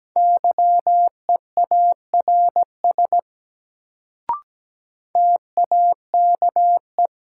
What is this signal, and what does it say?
Morse code, heard as a keyed tone, about 17 wpm, 700 Hz. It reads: YEARS TAKE
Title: Sentences from Top 200 Words - New Word Drill - 17wpm